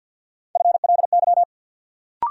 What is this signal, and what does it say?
Morse code, heard as a keyed tone, about 53 wpm, 700 Hz. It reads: FLY
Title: Sentences from Top 500 Words - New Word Drill - 50wpm